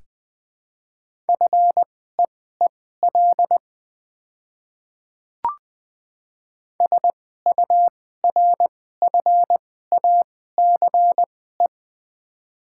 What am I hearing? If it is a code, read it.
Morse code, heard as a keyed tone, about 20 wpm, spaced as for 14 wpm, 700 Hz. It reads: FEEL SURFACE
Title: 2x Character Spacing - Top 500 Words - 20wpm